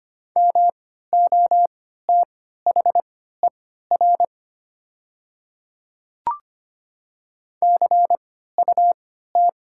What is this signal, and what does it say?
Morse code, heard as a keyed tone, about 25 wpm, spaced as for 14 wpm, 700 Hz. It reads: MOTHER CUT
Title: 3x Character Spacing - Top 300 Words - 25wpm